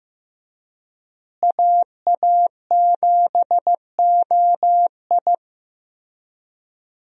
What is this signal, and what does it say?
Morse code, heard as a keyed tone, about 15 wpm, 700 Hz. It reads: AA7OI
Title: USA Call Signs - 5 Characters - Repeated 3x - 15wpm